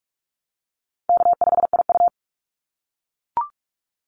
Morse code, heard as a keyed tone, about 44 wpm, 700 Hz. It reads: K5IU
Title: Call Signs - 45wpm